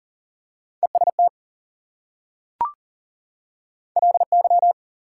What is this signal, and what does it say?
Morse code, heard as a keyed tone, about 40 wpm, 700 Hz. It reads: EST LY
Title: Prefixes and Suffixes 40wpm